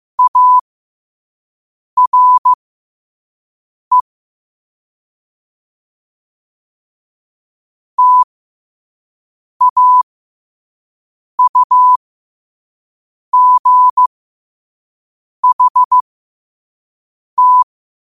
Morse code, heard as a keyed tone, about 15 wpm, spaced as for 5 wpm, 1000 Hz. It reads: ARE TAUGHT
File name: qotdfiles / 5wpm